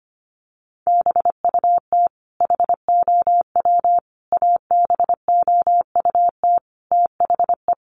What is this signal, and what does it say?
Morse code, heard as a keyed tone, about 25 wpm, 700 Hz. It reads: BUT HOW ABOUT THE
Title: Sentences from Top 1400 Words 25wpm